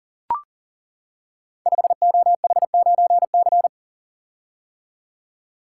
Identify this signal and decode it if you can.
Morse code, heard as a keyed tone, about 39 wpm, 700 Hz. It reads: 5OH9C
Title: Code Groups - 5 Letter:Numbers - 40wpm